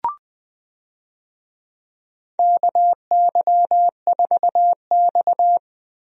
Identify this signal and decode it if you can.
Morse code, heard as a keyed tone, about 21 wpm, 700 Hz. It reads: KY4X
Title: USA Call Signs - 4 Characters - Repeated 3x - 20wpm